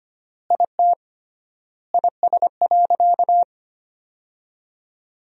Morse code, heard as a keyed tone, about 25 wpm, 700 Hz. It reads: IT IS.
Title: 3x Word-Spacing - Sentences from Top 500 Words - 25wpm